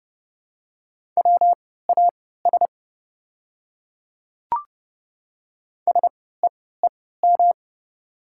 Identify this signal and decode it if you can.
Morse code, heard as a keyed tone, about 29 wpm, spaced as for 17 wpm, 700 Hz. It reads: WAS SEEM
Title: 3x Character Spacing - Top 300 Words - 30wpm